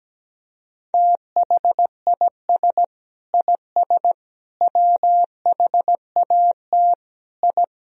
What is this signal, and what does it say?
Morse code, heard as a keyed tone, about 17 wpm, 700 Hz. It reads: THIS IS WHAT I
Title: Sentences from Top 1000 Words - Review Part 4 - 17wpm